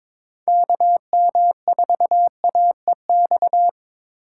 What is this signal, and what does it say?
Morse code, heard as a keyed tone, about 26 wpm, 700 Hz. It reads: KM4AEX
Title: Speed Racing - USA 2x3 Call Signs 22wpm